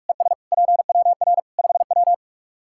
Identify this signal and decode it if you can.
Morse code, heard as a keyed tone, about 41 wpm, 700 Hz. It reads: ES PWR 5W
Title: Practice QSO - Exchange 2 - Info - 45wpm